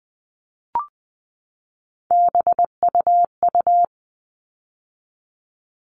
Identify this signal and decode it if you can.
Morse code, heard as a keyed tone, about 20 wpm, 700 Hz. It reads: BUU
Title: All Cut Numbers - 3-Digit 20wpm